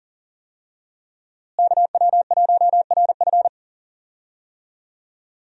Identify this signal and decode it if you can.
Morse code, heard as a keyed tone, about 39 wpm, 700 Hz. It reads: KW1RF
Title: USA Call Signs - 5 Characters - Repeated 2x - 40wpm